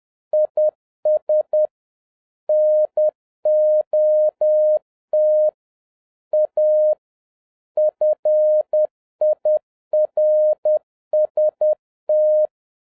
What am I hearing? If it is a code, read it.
Morse code, heard as a keyed tone, about 10 wpm, 610 Hz. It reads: IS NOT A FIRST
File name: 2011-07-mp3 / n21 GREETINGS FROM SINGAPORE 10wpm